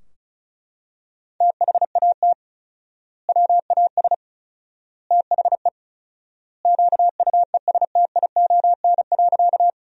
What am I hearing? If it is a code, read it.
Morse code, heard as a keyed tone, about 36 wpm, 700 Hz. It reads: THAT WAS THE QUESTION.
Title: 4x Word-Spacing - Sentences from Top 400 Words - 35wpm